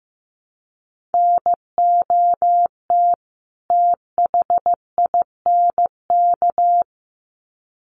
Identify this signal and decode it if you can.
Morse code, heard as a keyed tone, about 15 wpm, 700 Hz. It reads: NOT THINK
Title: Sets of 2 Words 15wpm